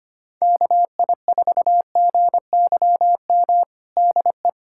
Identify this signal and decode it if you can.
Morse code, heard as a keyed tone, about 25 wpm, 700 Hz. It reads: KI4GYM DE
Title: USA Callsign DE Callsign - 6 Characters - 25wpm